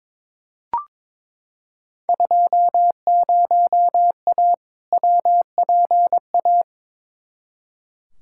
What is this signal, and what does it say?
Morse code, heard as a keyed tone, about 22 wpm, 700 Hz. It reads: 20A WPA
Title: Field Day Exchanges Encore - 22wpm